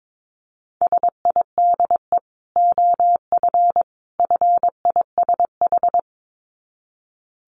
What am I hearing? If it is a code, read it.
Morse code, heard as a keyed tone, about 22 wpm, 700 Hz. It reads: SIDE OF FISH
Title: Dit Challenge 22wpm